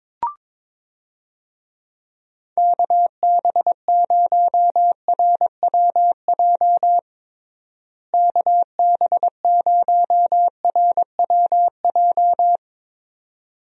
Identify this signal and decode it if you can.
Morse code, heard as a keyed tone, about 22 wpm, 700 Hz. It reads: KB0RWJ KB0RWJ
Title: USA Call Signs - 6 Characters - Repeated 2x - 22wpm